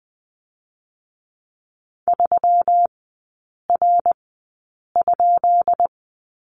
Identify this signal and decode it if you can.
Morse code, heard as a keyed tone, about 20 wpm, spaced as for 8 wpm, 690 Hz. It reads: 3R?
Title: 133 Lesson 22e - Question Mark - Three Characters 20wpm